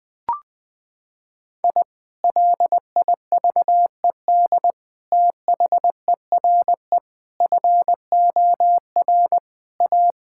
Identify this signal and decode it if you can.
Morse code, heard as a keyed tone, about 21 wpm, 700 Hz. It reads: I LIVED THERE FOR A
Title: Sentences with 3-Word Phrases - 20wpm